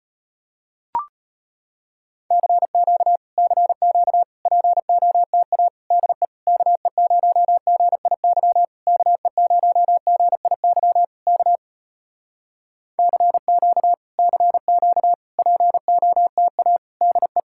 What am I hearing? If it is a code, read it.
Morse code, heard as a keyed tone, about 34 wpm, 700 Hz. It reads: CQ CQ POTA DE KE0GIY KE0GIY K CQ CQ POTA DE